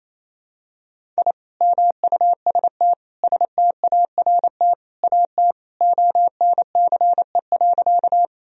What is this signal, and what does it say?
Morse code, heard as a keyed tone, about 27 wpm, 700 Hz. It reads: I MUST START AT ONCE.
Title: Sentences from Top 1000 Words - Review Part 1 - 28wpm